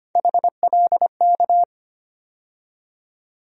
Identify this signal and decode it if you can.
Morse code, heard as a keyed tone, about 25 wpm, 700 Hz. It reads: HLK
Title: USA Call Sign Postfixes - 3 Letters 25wpm